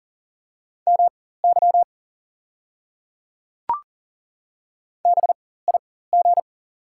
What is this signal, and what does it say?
Morse code, heard as a keyed tone, about 39 wpm, spaced as for 18 wpm, 700 Hz. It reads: MY BIG
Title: 4x Character Spacing - Top 300 Words - 40wpm